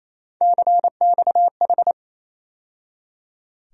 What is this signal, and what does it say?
Morse code, heard as a keyed tone, about 28 wpm, 700 Hz. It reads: CXH
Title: USA Call Sign Postfixes - 3 Letters 28wpm